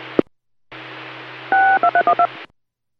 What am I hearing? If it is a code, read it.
DTMF, heard as telephone touch tones, tones of about 80 ms. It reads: 62312